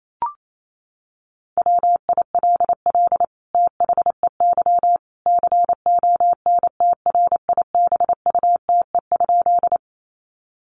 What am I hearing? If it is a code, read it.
Morse code, heard as a keyed tone, about 28 wpm, 700 Hz. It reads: WILL THEY CONTRIBUTE?
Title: Sentences from Top 1400 Words 28wpm